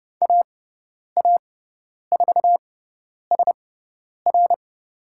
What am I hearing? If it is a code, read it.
Morse code, heard as a keyed tone, about 30 wpm, spaced as for 10 wpm, 700 Hz. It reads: AA4SR
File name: Lesson 16g - Letter c - Call Signs 30wpm